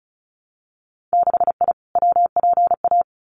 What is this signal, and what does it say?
Morse code, heard as a keyed tone, about 36 wpm, 700 Hz. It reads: 6I WPA
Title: Winter Field Day Exchanges Encore 35wpm